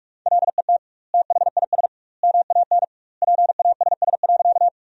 Morse code, heard as a keyed tone, about 41 wpm, 700 Hz. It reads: LET THIS MAN PASS.
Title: 2x Word-Spacing - Sentences from Top 400 Words - 45wpm